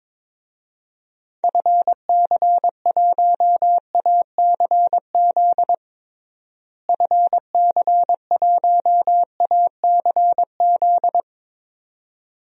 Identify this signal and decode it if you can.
Morse code, heard as a keyed tone, about 22 wpm, 700 Hz. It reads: FC1ACZ FC1ACZ
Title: DXCC-100 Call Signs - 6 Characters - Repeated 2x - 22wpm